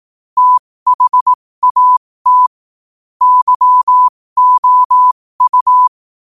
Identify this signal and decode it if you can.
Morse code, heard as a keyed tone, about 18 wpm, spaced as for 15 wpm, 1000 Hz. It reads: THAT YOU
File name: qotdfiles / 15wpm